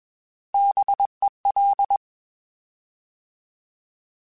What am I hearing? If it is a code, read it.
Morse code, heard as a keyed tone, about 21 wpm, 800 Hz. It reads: BEL
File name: Alphabetical / Alphabetical mp3s / Bs @20